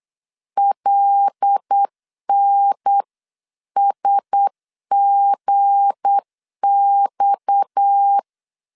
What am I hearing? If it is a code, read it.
Morse code, heard as a keyed tone, about 8 wpm, 800 Hz. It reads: LNSGX
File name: CW / DET / 1 9 1500C S C 8wpm 5653